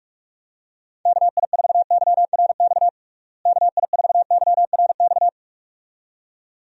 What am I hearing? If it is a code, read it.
Morse code, heard as a keyed tone, about 46 wpm, 700 Hz. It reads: KI4YRX KI4YRX
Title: USA Call Signs - 6 Characters - Repeated 2x - 45wpm